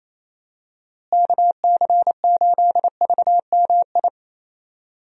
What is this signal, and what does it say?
Morse code, heard as a keyed tone, about 28 wpm, 700 Hz. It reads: KC8VMI
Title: USA Call Signs - 6 Characters - Repeated 3x - 28wpm